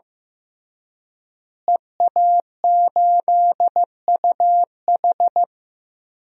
Morse code, heard as a keyed tone, about 15 wpm, 700 Hz. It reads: EA8UH
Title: DXCC-100 Call Signs - 5 Characters - Repeated 2x - 15wpm